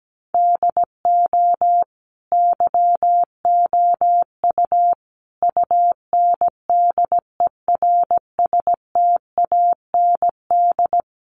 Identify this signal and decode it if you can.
Morse code, heard as a keyed tone, about 17 wpm, 700 Hz. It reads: DO YOU UNDERSTAND
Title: Sentences from Top 600 Words 17wpm